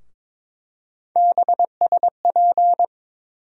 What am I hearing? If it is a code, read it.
Morse code, heard as a keyed tone, about 22 wpm, 700 Hz. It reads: BSP
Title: USA Call Sign Postfixes - 3 Letters 22wpm